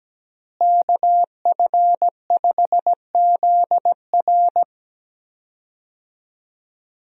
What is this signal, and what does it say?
Morse code, heard as a keyed tone, about 17 wpm, 700 Hz. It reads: KF5ZR
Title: USA Call Signs - 5 Characters - Repeated 2x - 17wpm